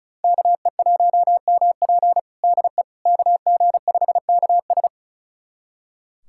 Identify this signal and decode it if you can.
Morse code, heard as a keyed tone, about 36 wpm, 700 Hz. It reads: KE1MP DE KG5KS
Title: USA Callsign DE Callsign - 5 Characters - 35wpm